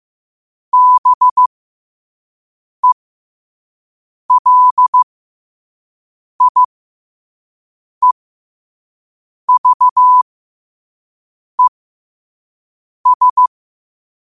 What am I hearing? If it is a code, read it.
Morse code, heard as a keyed tone, about 15 wpm, spaced as for 5 wpm, 1000 Hz. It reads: BELIEVES
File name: qotdfiles / 5wpm